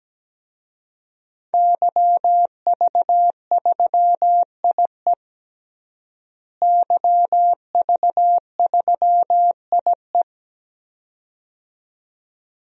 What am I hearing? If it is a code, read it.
Morse code, heard as a keyed tone, about 17 wpm, 700 Hz. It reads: YV3IE YV3IE
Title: DXCC-100 Call Signs - 5 Characters - Repeated 2x - 17wpm